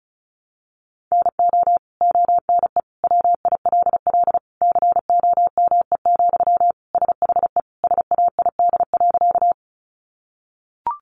Morse code, heard as a keyed tone, about 34 wpm, 700 Hz. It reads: NO ONE WILL COME, SHE SAID.
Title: Sentences from Top 100 Words 35wpm